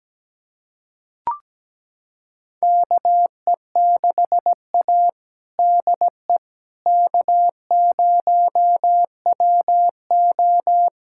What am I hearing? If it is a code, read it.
Morse code, heard as a keyed tone, about 17 wpm, 700 Hz. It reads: KE6A DE K0WO
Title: USA Callsign DE Callsign - 4 Characters - 17wpm